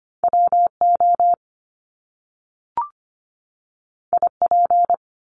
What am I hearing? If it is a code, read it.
Morse code, heard as a keyed tone, about 25 wpm, 700 Hz. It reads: WO IP
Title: Top Two Letter Combinations in Words 25wpm